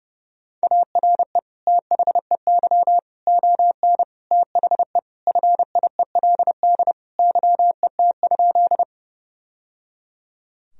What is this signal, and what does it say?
Morse code, heard as a keyed tone, about 30 wpm, 700 Hz. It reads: ARE THEY ON THE FIELD YET?
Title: Sentences from Top 1000 Words - Review Part 2 - 30wpm